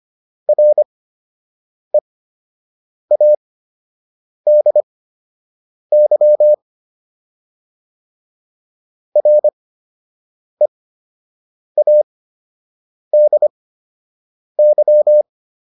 Morse code, heard as a keyed tone, about 25 wpm, spaced as for 7 wpm, 600 Hz. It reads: READY READY